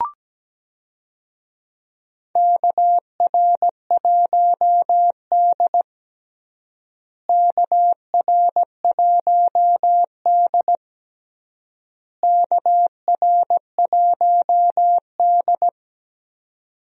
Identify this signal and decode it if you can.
Morse code, heard as a keyed tone, about 17 wpm, 700 Hz. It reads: KR1D KR1D KR1D
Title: USA Call Signs - 4 Characters - Repeated 3x - 17wpm